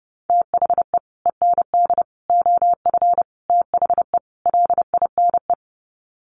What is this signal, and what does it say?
Morse code, heard as a keyed tone, about 30 wpm, 700 Hz. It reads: THE END OF THE LINE
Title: Sets of 5 Words Encore2 30wpm